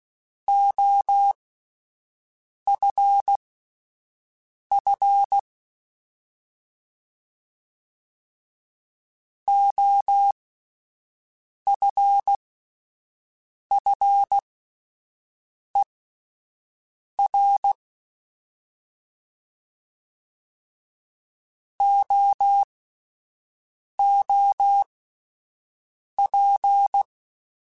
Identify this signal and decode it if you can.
Morse code, heard as a keyed tone, about 16 wpm, spaced as for 5 wpm, 800 Hz. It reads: OFF OFFER OOP
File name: Alphabetical / Alphabetical mp3s / Os @5